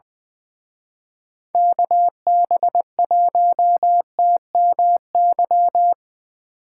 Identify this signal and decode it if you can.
Morse code, heard as a keyed tone, about 23 wpm, 700 Hz. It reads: KB1TMY
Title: Speed Racing - USA 2x3 Call Signs 20wpm